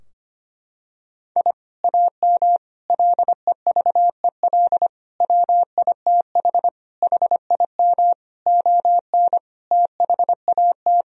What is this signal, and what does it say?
Morse code, heard as a keyed tone, about 25 wpm, 700 Hz. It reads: I AM LEVEL WITH HIM ON THAT
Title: Sentences from Top 1000 Words - Review Part 4 - 25wpm